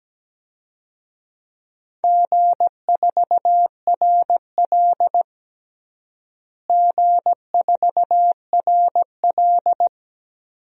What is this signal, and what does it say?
Morse code, heard as a keyed tone, about 17 wpm, 700 Hz. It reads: G4RL G4RL
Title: DXCC-100 Call Signs - 4 Characters - Repeated 3x - 17wpm